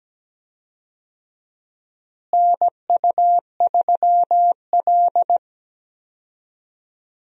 Morse code, heard as a keyed tone, about 17 wpm, 700 Hz. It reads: NU3L
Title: USA Call Signs - 4 Characters - Repeated 3x - 17wpm